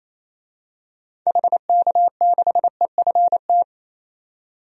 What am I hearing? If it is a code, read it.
Morse code, heard as a keyed tone, about 28 wpm, 700 Hz. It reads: HK6EFT
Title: DXCC-100 Call Signs - 6 Characters - 28wpm